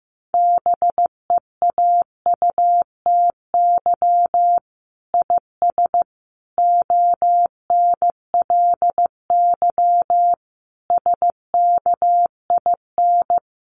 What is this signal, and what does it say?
Morse code, heard as a keyed tone, about 15 wpm, 700 Hz. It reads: BEAUTY IS ONLY SKIN
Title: Sentences from Top 1300 Words 15wpm